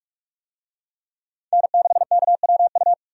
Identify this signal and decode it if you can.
Morse code, heard as a keyed tone, about 47 wpm, 700 Hz. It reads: N6KWU
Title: USA Call Signs - 5 Characters - 45wpm